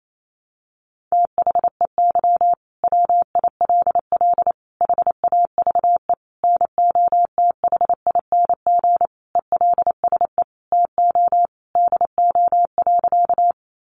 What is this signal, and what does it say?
Morse code, heard as a keyed tone, about 28 wpm, 700 Hz. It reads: THEY WILL HAVE NOTHING ELSE TO DO.
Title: Sentences from Top 800 Words 28wpm